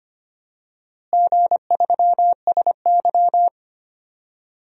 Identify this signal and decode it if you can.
Morse code, heard as a keyed tone, about 25 wpm, 700 Hz. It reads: G3SY
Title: DXCC-100 Call Signs - 4 Characters - Repeated 3x - 25wpm